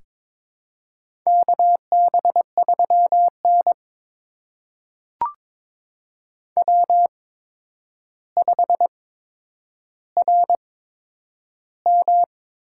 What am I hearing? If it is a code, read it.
Morse code, heard as a keyed tone, about 22 wpm, spaced as for 9 wpm, 700 Hz. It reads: KB3N W5RM
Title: USA Call Signs - 4 Characters - 8x Character Spacing - 22wpm